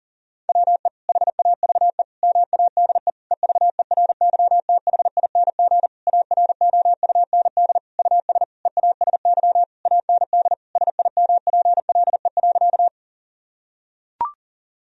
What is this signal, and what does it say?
Morse code, heard as a keyed tone, about 37 wpm, 700 Hz. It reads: WE HAVE MADE EVERYTHING AROUND US EASY AND SIMPLE.
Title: Sentences from Top 1000 Words - Review Part 3 - 40wpm